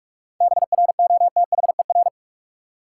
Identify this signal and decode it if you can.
Morse code, heard as a keyed tone, about 46 wpm, 700 Hz. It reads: BROTHER
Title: Seven Letter Words - 45wpm